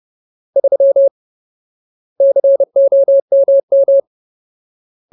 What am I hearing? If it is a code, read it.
Morse code, heard as a keyed tone, about 30 wpm, 550 Hz. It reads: 3 COMM
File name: Ultimate Word List / 30WPM Ultimate